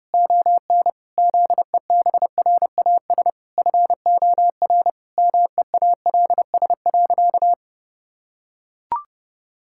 Morse code, heard as a keyed tone, about 30 wpm, 700 Hz. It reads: ON ZEBRAS FOR MEALS.
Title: Pangram Challenge v2 - 30wpm